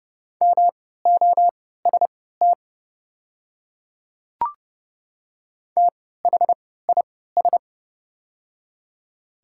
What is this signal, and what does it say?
Morse code, heard as a keyed tone, about 30 wpm, spaced as for 17 wpm, 700 Hz. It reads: MOST THIS
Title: 3x Character Spacing - Top 100 Words - 30wpm